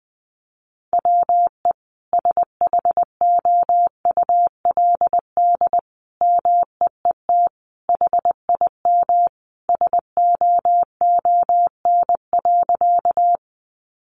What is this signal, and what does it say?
Morse code, heard as a keyed tone, about 20 wpm, 700 Hz. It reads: WE SHOULD MEET HIM SOON.
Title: Sentences from Top 1100 Words 20wpm